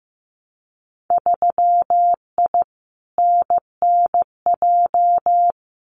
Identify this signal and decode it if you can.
Morse code, heard as a keyed tone, about 15 wpm, 700 Hz. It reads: 3I NNJ
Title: Winter Field Day Exchanges 15wpm